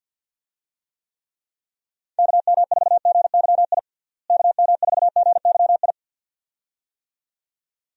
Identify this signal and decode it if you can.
Morse code, heard as a keyed tone, about 52 wpm, 700 Hz. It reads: KM4GYI KM4GYI
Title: USA Call Signs - 6 Characters - Repeated 2x - 50wpm